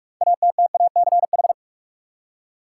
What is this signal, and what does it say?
Morse code, heard as a keyed tone, about 46 wpm, 700 Hz. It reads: ATTACH
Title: Six Letter Words - 45wpm